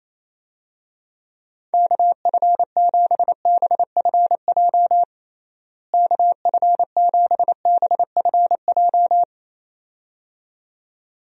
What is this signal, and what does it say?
Morse code, heard as a keyed tone, about 28 wpm, 700 Hz. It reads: KF7BFJ KF7BFJ
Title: USA Call Signs - 6 Characters - Repeated 2x - 28wpm